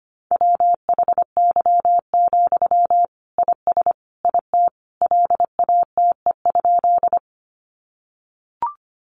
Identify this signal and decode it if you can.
Morse code, heard as a keyed tone, about 25 wpm, 700 Hz. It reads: WHY, IS IT LATE?
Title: Sentences from Top 300 Words 25wpm